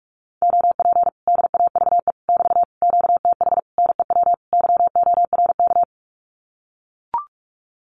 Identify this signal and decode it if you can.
Morse code, heard as a keyed tone, about 44 wpm, 700 Hz. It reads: OP DAVE <BT> QTH NEW YORK
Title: Common QSO Phrases Encore 45wpm